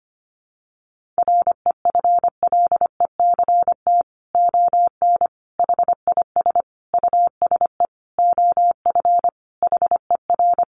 Morse code, heard as a keyed tone, about 25 wpm, 700 Hz. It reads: REFLECT ON HIS USE OF HER
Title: Sentences from Top 1500 Words 25wpm